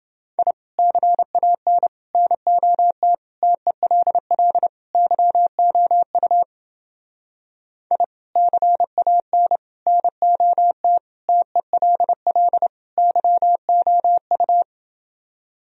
Morse code, heard as a keyed tone, about 28 wpm, 700 Hz. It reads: I CAN NOT TELL YOU I CAN NOT TELL YOU